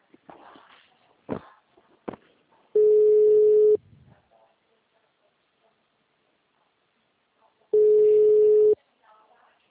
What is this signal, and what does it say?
Ringback tone (Europe, 425 Hz), with speech.